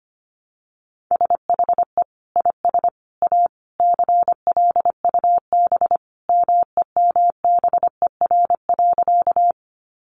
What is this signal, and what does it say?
Morse code, heard as a keyed tone, about 25 wpm, 700 Hz. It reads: SHE IS A CLUB MEMBER.
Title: Sentences from Top 1200 Words 25wpm